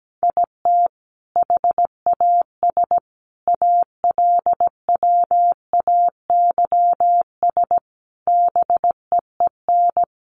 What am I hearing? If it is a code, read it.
Morse code, heard as a keyed tone, about 17 wpm, 700 Hz. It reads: IT HAS ALWAYS BEEN
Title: Sentences from Top 1300 Words 17wpm